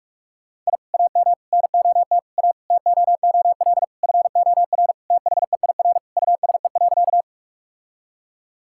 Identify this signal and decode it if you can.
Morse code, heard as a keyed tone, about 41 wpm, 700 Hz. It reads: I AM NOT A TOOL FOR THEIR USE.
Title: Sentences from Top 1000 Words - Review Part 6 - 45wpm